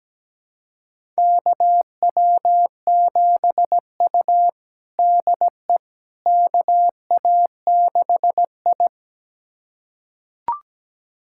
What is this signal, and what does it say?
Morse code, heard as a keyed tone, about 17 wpm, 700 Hz. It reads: KW7U DE KA6I
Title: USA Callsign DE Callsign - 4 Characters - 17wpm